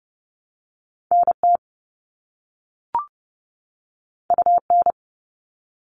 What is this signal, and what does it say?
Morse code, heard as a keyed tone, about 30 wpm, 700 Hz. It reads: NT UN
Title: All Cut Numbers - 2-Digit 30wpm